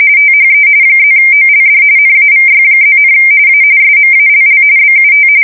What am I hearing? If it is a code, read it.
RTTY, about 45.45 baud. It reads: THE +OLD NAVY+, A SMALL BUT
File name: fsk170 / usnavy1